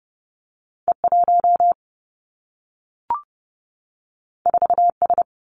Morse code, heard as a keyed tone, about 30 wpm, 700 Hz. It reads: E1 4S
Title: Two Challenging Letter-Number Combinations 30wpm